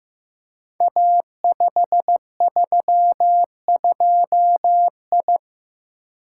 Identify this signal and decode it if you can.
Morse code, heard as a keyed tone, about 15 wpm, 700 Hz. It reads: A532I
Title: Code Groups - 5 Letter:Numbers - 15wpm